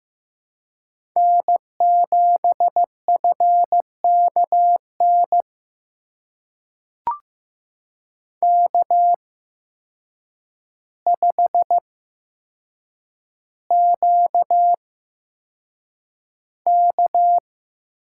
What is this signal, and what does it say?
Morse code, heard as a keyed tone, about 15 wpm, spaced as for 5 wpm, 700 Hz. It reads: N7FK NK5QK
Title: USA Call Signs - 8x Character Spacing - 15wpm